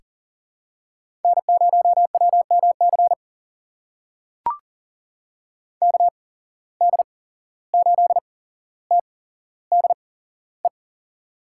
Morse code, heard as a keyed tone, about 40 wpm, spaced as for 16 wpm, 700 Hz. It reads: N0WMC KD8TDE
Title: USA Call Signs - 8x Character Spacing - 40wpm